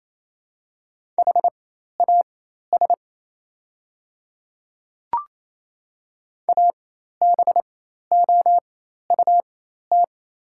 Morse code, heard as a keyed tone, about 27 wpm, spaced as for 13 wpm, 690 Hz. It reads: HAS ABOUT
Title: 4x Character Spacing - Top 200 Words - 28wpm